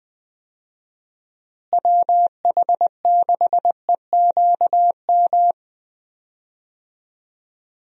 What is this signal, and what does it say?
Morse code, heard as a keyed tone, about 20 wpm, 700 Hz. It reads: WH6EQM
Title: USA Call Signs - 6 Characters - Repeated 2x - 20wpm